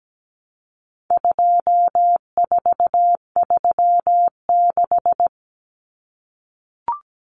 Morse code, heard as a keyed tone, about 17 wpm, 700 Hz. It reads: 2436
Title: Numbers - 4-Digit 17wpm